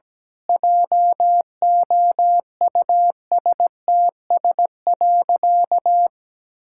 Morse code, heard as a keyed tone, about 17 wpm, 700 Hz. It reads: JOUSTS.
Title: Pangram Challenge v2 - 17wpm